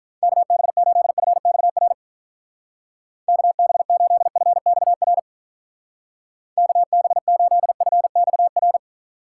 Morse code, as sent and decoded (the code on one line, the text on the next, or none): -.- -... ---.. ..-. -..- .-. / -.- -... ---.. ..-. -..- .-. / -.- -... ---.. ..-. -..- .-.
KB8FXR KB8FXR KB8FXR